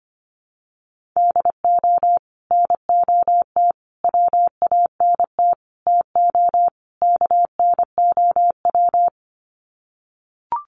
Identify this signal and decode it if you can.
Morse code, heard as a keyed tone, about 25 wpm, 700 Hz. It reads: DO NOT WANT TO KNOW
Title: Sets of 5 Words Encore 25wpm